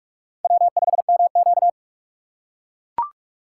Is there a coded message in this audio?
Yes: Morse code from 0.4 to 1.7 s, heard as a keyed tone, about 43 wpm, 700 Hz. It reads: W5MQ